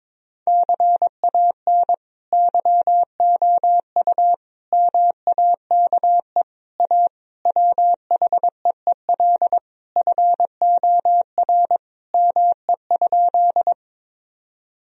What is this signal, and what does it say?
Morse code, heard as a keyed tone, about 22 wpm, 700 Hz. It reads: CAN YOU MAKE A WHEEL FOR ME?
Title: Sentences from Top 1000 Words - Review Part 3 - 22wpm